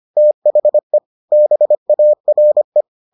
Morse code, heard as a keyed tone, about 25 wpm, 600 Hz. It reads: THE BARE